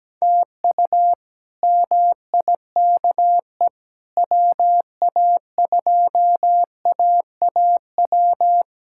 Morse code, heard as a keyed tone, about 17 wpm, 700 Hz. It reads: TU MIKE WA2AAW
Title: K1USN SST QSO Practice - 17wpm